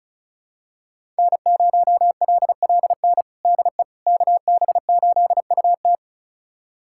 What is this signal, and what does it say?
Morse code, heard as a keyed tone, about 36 wpm, 700 Hz. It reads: N0LLN DE KB8UT
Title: USA Callsign DE Callsign - 5 Characters - 35wpm